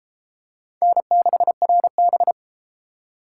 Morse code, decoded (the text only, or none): N6RB